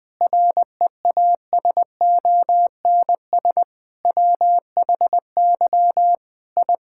Morse code, heard as a keyed tone, about 20 wpm, 700 Hz. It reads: REASONS WHY I
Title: Sentences with 7-Word Phrases - Spoken Beginning - 20wpm